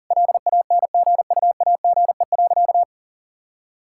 Morse code, heard as a keyed tone, about 40 wpm, 700 Hz. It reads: LANGUAGE.
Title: Sentences from Top 1600 Words 40wpm